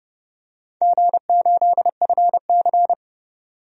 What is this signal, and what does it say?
Morse code, heard as a keyed tone, about 30 wpm, 700 Hz. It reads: G8FC
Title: DXCC-100 Call Signs - 4 Characters - 30wpm